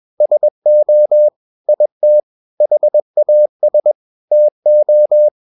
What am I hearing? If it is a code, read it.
Morse code, heard as a keyed tone, about 21 wpm, 600 Hz. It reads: SO IT HAS TO
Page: CW med Gnister